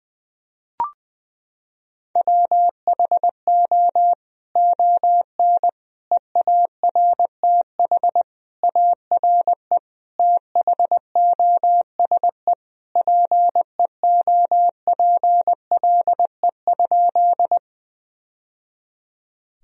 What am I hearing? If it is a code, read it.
Morse code, heard as a keyed tone, about 20 wpm, 700 Hz. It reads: WHO ON EARTH ARE THOSE PEOPLE?
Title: Sentences from Top 1000 Words - Review Part 1 - 20wpm